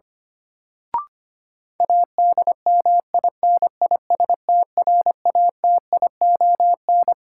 Morse code, heard as a keyed tone, about 25 wpm, 700 Hz. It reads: ADMINISTRATION
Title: Sets of 5 Words - New Word Drill - 25wpm